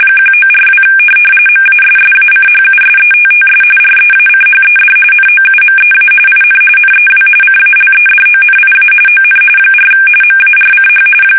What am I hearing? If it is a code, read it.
RTTY, about 45.45 baud. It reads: 1942 MASSACHUSETTS REACHED NOUMEA IN NEW CALEDONIA ON 4 MARCH
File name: fsk850 / USS-Massachusetts